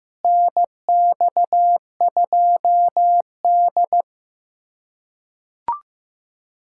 Morse code, heard as a keyed tone, about 15 wpm, 700 Hz. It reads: NX2D
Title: USA Callsign DE Callsign - 4 Characters - 15wpm